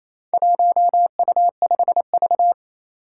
Morse code, heard as a keyed tone, about 28 wpm, 700 Hz. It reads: 1U5V
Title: Four Challenging Letter-Number Combinations 28wpm